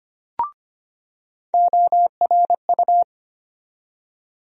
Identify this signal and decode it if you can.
Morse code, heard as a keyed tone, about 25 wpm, 700 Hz. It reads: ORU
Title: USA Call Sign Postfixes - 3 Letters 25wpm